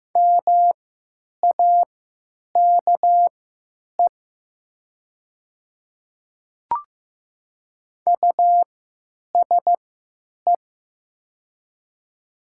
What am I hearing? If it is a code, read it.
Morse code, heard as a keyed tone, about 15 wpm, spaced as for 9 wpm, 700 Hz. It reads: MAKE USE
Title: 3x Character Spacing - Top 200 Words - 15wpm